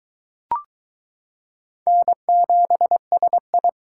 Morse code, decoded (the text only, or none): N7SI